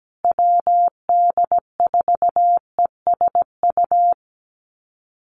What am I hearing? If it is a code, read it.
Morse code, heard as a keyed tone, about 17 wpm, 700 Hz. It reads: WD4ESU
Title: USA Call Signs Repeated 3x 17wpm